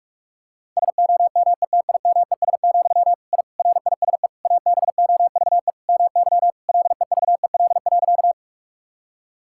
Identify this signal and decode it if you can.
Morse code, heard as a keyed tone, about 46 wpm, 700 Hz. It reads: SOMETIMES, I RISE ABOVE MY LEVEL.
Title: Sentences from Top 1000 Words - Review Part 6 - 45wpm